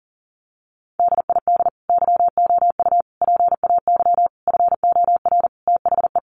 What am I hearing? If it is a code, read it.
Morse code, heard as a keyed tone, about 40 wpm, 700 Hz. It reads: DID YOU PAY FOR THE
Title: Sentences from Top 1500 Words 40wpm